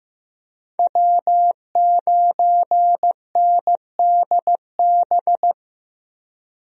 Morse code, heard as a keyed tone, about 16 wpm, 700 Hz. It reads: W9NDB
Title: USA Call Signs - 5 Characters - Repeated 3x - 15wpm